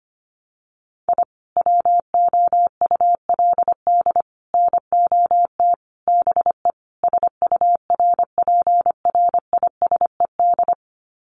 Morse code, heard as a keyed tone, about 25 wpm, 700 Hz. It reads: I WOULD NOT BE SURPRISED
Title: Sets of 5 Words Encore2 25wpm